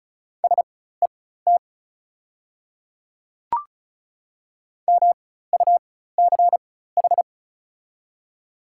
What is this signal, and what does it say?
Morse code, heard as a keyed tone, about 34 wpm, spaced as for 16 wpm, 700 Hz. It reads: SET MUCH
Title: 4x Character Spacing - Top 200 Words - 35wpm